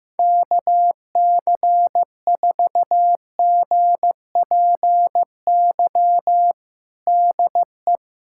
Morse code, heard as a keyed tone, about 15 wpm, 700 Hz. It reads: KC4GPY DE
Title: USA Callsign DE Callsign - 6 Characters - 15wpm